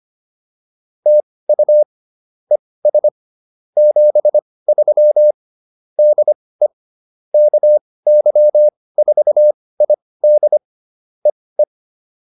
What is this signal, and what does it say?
Morse code, heard as a keyed tone, about 25 wpm, spaced as for 18 wpm, 600 Hz. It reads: TU ES 73 DE KY4ID EE